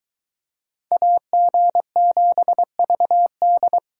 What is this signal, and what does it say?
Morse code, heard as a keyed tone, about 22 wpm, 700 Hz. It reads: AG7VD